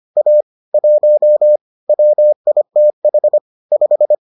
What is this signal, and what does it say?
Morse code, heard as a keyed tone, about 25 wpm, 600 Hz. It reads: A 1 WITH 5